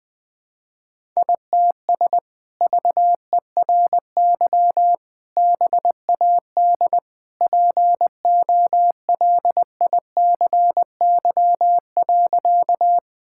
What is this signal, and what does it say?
Morse code, heard as a keyed tone, about 20 wpm, 700 Hz. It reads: ITS VERY BAD POLICY.
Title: Sentences from Top 1000 Words - Review Part 5 - 20wpm